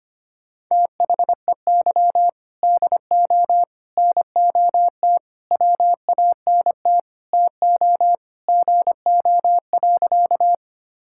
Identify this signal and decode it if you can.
Morse code, heard as a keyed tone, about 25 wpm, 700 Hz. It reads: THEY DO NOT WANT TO GO.
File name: practice / 25wpm / Sentences with 4-Word Phrases - 25wpm